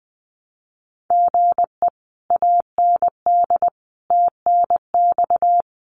Morse code, heard as a keyed tone, about 20 wpm, 700 Hz. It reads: GE AND TNX
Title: Common QSO Phrases Encore 20wpm